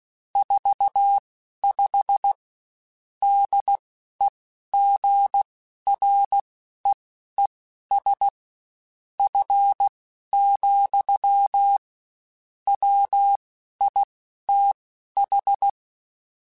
Morse code, heard as a keyed tone, about 16 wpm, spaced as for 12 wpm, 800 Hz. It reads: 45 DEGREES F, WITH
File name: Spaced Characters / Spaced Characters-mp3s / Weather sp @15